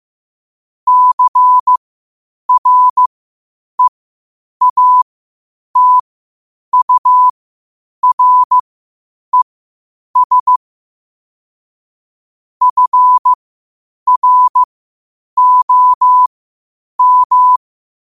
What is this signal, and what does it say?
Morse code, heard as a keyed tone, about 15 wpm, spaced as for 8 wpm, 1000 Hz. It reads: CREATURES FROM
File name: qotdfiles / 8wpm